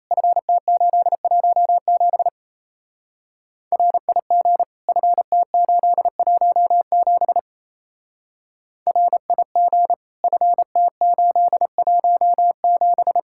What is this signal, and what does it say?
Morse code, heard as a keyed tone, about 32 wpm, 700 Hz. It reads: FT817 RIG FT817 RIG FT817
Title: Speed-Racing Common QSO Phrases Encore - 28wpm